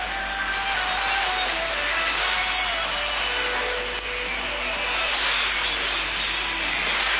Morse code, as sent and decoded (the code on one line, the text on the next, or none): none